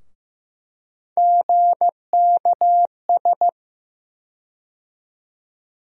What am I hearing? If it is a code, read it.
Morse code, heard as a keyed tone, about 15 wpm, 700 Hz. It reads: GKS
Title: USA Call Sign Postfixes - 3 Letters 15wpm